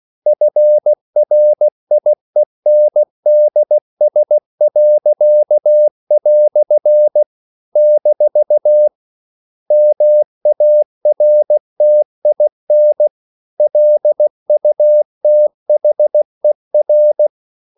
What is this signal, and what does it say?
Morse code, heard as a keyed tone, about 16 wpm, 600 Hz. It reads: FRIENDS." - MARTIN LUTHER
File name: CW / Practice / EnglishQuotations0116